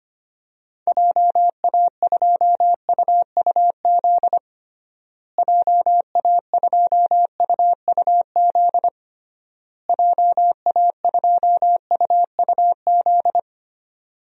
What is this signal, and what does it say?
Morse code, heard as a keyed tone, about 25 wpm, 700 Hz. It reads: JA2UUZ JA2UUZ JA2UUZ
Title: DXCC-100 Call Signs - 6 Characters - Repeated 3x - 25wpm